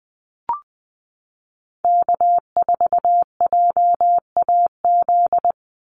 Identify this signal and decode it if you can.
Morse code, heard as a keyed tone, about 20 wpm, 700 Hz. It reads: K4JAZ
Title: Call Signs Repeated 3x - 20wpm